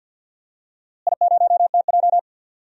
Morse code, heard as a keyed tone, about 52 wpm, 700 Hz. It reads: I0TJ
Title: DXCC-100 Call Signs - 4 Characters - Repeated 3x - 50wpm